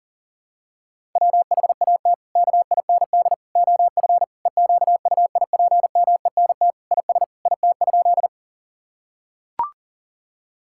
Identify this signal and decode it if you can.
Morse code, heard as a keyed tone, about 40 wpm, 700 Hz. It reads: WHAT KIND OF EQUIPMENT IS IT?
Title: Sentences from Top 1000 Words - Review Part 5 - 40wpm